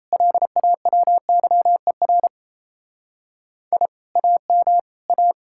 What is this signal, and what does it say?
Morse code, heard as a keyed tone, about 36 wpm, 700 Hz. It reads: LAWYER I AM A